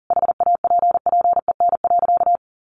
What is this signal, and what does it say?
Morse code, heard as a keyed tone, about 40 wpm, 700 Hz. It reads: HAPPEN.
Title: Sentences from Top 400 Words 40wpm